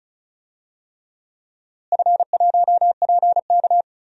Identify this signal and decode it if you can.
Morse code, heard as a keyed tone, about 36 wpm, 700 Hz. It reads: F1PK
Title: DXCC-100 Call Signs - 4 Characters - Repeated 3x - 35wpm